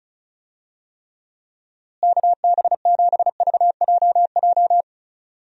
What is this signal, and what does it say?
Morse code, heard as a keyed tone, about 36 wpm, 700 Hz. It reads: KB7VJJ
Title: USA Call Signs - 6 Characters - Repeated 3x - 35wpm